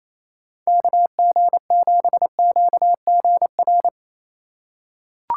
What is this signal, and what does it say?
Morse code, heard as a keyed tone, about 27 wpm, 700 Hz. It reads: KG7QGR
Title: USA Call Signs - 4x Character Spacing - 28wpm